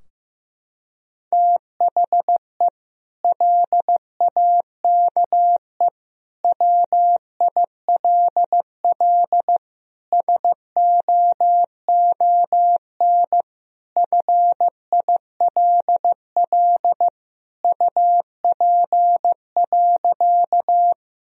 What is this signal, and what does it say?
Morse code, heard as a keyed tone, about 16 wpm, 700 Hz. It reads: THE LAKE WILL SOON FILL UP.
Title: Sentences from Top 1000 Words - Review Part 3 - 15wpm